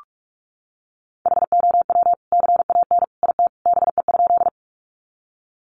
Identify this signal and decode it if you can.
Morse code, heard as a keyed tone, about 44 wpm, 700 Hz. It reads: HOW CAN IT BE?
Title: Sentences from Top 100 Words 45wpm